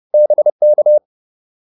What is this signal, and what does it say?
Morse code, heard as a keyed tone, about 30 wpm, 600 Hz. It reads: BK